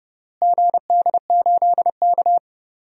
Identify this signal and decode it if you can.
Morse code, heard as a keyed tone, about 31 wpm, 700 Hz. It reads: GD8K
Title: DXCC-100 Call Signs - 4 Characters - Repeated 3x - 30wpm